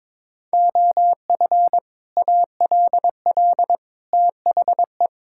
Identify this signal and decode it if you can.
Morse code, heard as a keyed tone, about 22 wpm, 700 Hz. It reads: OF ALL THE
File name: practice / 22wpm / Sentences with 3-Word Phrases - 22wpm